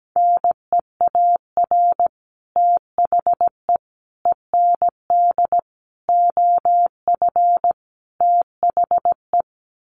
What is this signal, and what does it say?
Morse code, heard as a keyed tone, about 17 wpm, 700 Hz. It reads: NEAR THE END OF THE
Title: Sets of 5 Words Encore 17wpm